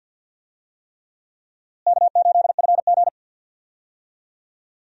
Morse code, heard as a keyed tone, about 51 wpm, 700 Hz. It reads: K8FG
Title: USA Call Signs - 4 Characters - Repeated 3x - 50wpm